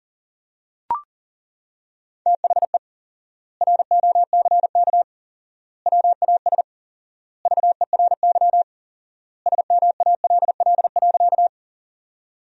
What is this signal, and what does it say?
Morse code, heard as a keyed tone, about 40 wpm, 700 Hz. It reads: THE ROCK WAS VERY SMALL.
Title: 4x Word-Spacing - Sentences from Top 400 Words - 40wpm